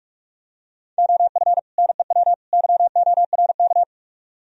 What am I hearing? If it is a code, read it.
Morse code, heard as a keyed tone, about 49 wpm, 700 Hz. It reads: OF NEW YORK